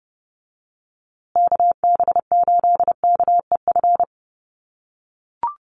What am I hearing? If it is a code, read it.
Morse code, heard as a keyed tone, about 30 wpm, 700 Hz. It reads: KB8KEF
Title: USA Call Signs Repeated 3x 30wpm